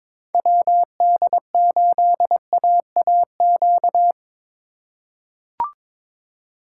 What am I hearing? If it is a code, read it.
Morse code, heard as a keyed tone, about 22 wpm, 700 Hz. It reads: WD8AAQ
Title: USA Call Signs - 6 Characters - 3x Character Spacing - 22wpm